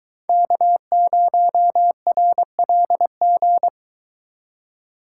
Morse code, heard as a keyed tone, about 22 wpm, 700 Hz. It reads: K0RLG